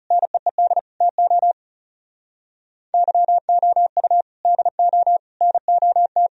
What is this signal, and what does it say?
Morse code, heard as a keyed tone, about 37 wpm, 700 Hz. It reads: NEED TO YOU DO NOT
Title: Speed-Racing - Sets of 5 Words 30wpm